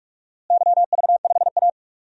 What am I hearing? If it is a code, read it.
Morse code, heard as a keyed tone, about 46 wpm, 700 Hz. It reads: YV5A
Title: DXCC-100 Call Signs - 4 Characters - Repeated 2x - 45wpm